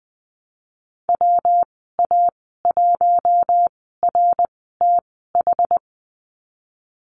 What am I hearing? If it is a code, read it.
Morse code, heard as a keyed tone, about 20 wpm, spaced as for 15 wpm, 700 Hz. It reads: WA1RTH
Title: 140 Lesson 23g - Letter f - Call Signs 20wpm